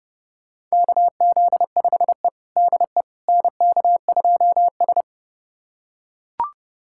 Morse code, heard as a keyed tone, about 32 wpm, 700 Hz. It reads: KZ5E DE NK2S
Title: USA Callsign DE Callsign - 4 Characters - 30wpm